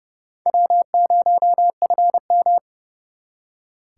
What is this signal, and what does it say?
Morse code, heard as a keyed tone, about 38 wpm, 700 Hz. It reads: W0FM